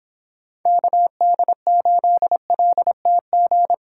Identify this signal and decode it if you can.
Morse code, heard as a keyed tone, about 24 wpm, 700 Hz. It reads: KD8LTG